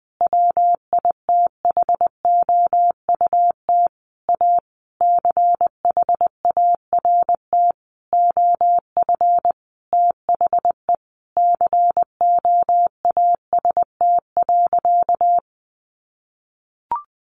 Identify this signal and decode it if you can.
Morse code, heard as a keyed tone, about 20 wpm, 700 Hz. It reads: WITHOUT A CHART OF THE COAST.
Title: Sentences from Top 1300 Words 20wpm